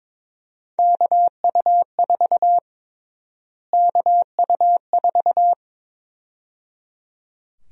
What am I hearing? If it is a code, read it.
Morse code, heard as a keyed tone, about 22 wpm, 700 Hz. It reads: KU4 KU4
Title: USA Call Sign Prefixes - Repeated 2x - 22wpm